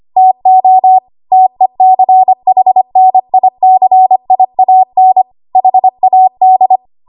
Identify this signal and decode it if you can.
Morse code, heard as a keyed tone, about 25 wpm, 750 Hz. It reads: TO TECHNICIAN HAD
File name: Archive / 25 WPM / 25WPM